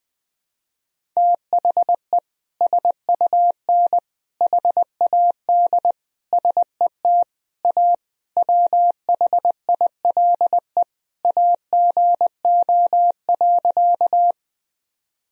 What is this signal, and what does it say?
Morse code, heard as a keyed tone, about 21 wpm, 700 Hz. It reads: THE SUN HAD SET A WHILE AGO.
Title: Sentences from Top 1000 Words - Review Part 1 - 20wpm